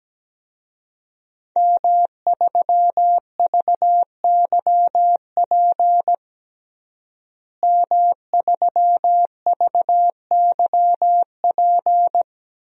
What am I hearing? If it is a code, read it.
Morse code, heard as a keyed tone, about 17 wpm, 700 Hz. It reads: M3VYP M3VYP
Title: DXCC-100 Call Signs - 5 Characters - Repeated 3x - 17wpm